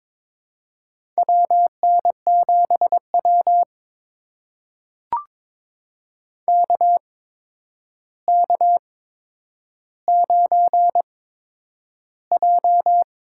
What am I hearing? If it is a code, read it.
Morse code, heard as a keyed tone, about 22 wpm, spaced as for 9 wpm, 700 Hz. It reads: WN7W KK9J
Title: USA Call Signs - 4 Characters - 8x Character Spacing - 22wpm